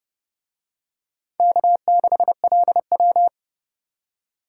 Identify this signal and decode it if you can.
Morse code, heard as a keyed tone, about 28 wpm, 700 Hz. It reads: K6LW